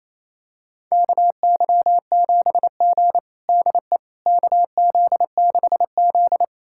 Morse code, heard as a keyed tone, about 28 wpm, 700 Hz. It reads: KY7G DE KZ6Z
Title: USA Callsign DE Callsign - 4 Characters - 28wpm